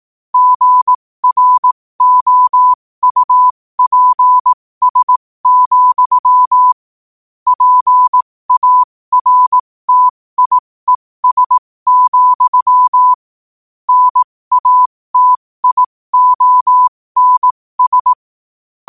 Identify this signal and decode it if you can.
Morse code, heard as a keyed tone, about 18 wpm, spaced as for 15 wpm, 1000 Hz. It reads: GROUPS, PARTIES, NATIONS
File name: qotdfiles / 15wpm